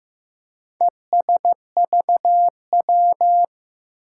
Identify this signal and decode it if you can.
Morse code, heard as a keyed tone, about 15 wpm, 700 Hz. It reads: ESVW
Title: Four Challenging Letter-Number Combinations 15wpm